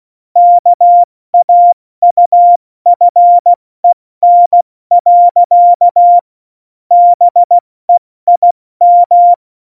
Morse code, heard as a keyed tone, about 16 wpm, 700 Hz. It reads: KAUFEN. BEIM
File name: Morsekurs / Witze / Witz 26 Mai, 2020 70 F80